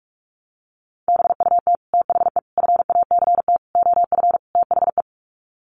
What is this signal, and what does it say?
Morse code, heard as a keyed tone, about 44 wpm, 700 Hz. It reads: BUT THE FACT OF THE
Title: Sets of 5 Words Encore2 45wpm